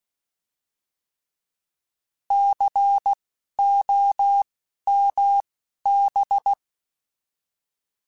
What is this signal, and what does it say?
Morse code, heard as a keyed tone, about 16 wpm, spaced as for 12 wpm, 800 Hz. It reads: COMB
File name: Alphabetical / Alphabetical mp3s / Cs @10